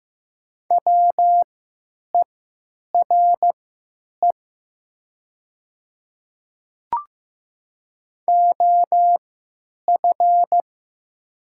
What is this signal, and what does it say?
Morse code, heard as a keyed tone, about 15 wpm, spaced as for 8 wpm, 700 Hz. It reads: WERE OF
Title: 3x Character Spacing - Top 200 Words - 15wpm